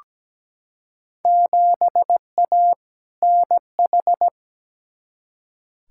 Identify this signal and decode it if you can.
Morse code, heard as a keyed tone, about 17 wpm, 700 Hz. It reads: 7A NH
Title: Field Day Exchanges - 17wpm